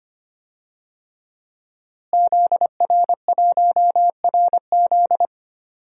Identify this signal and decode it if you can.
Morse code, heard as a keyed tone, about 25 wpm, 700 Hz. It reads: ZR1RZ
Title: DXCC-100 Call Signs - 5 Characters - Repeated 3x - 25wpm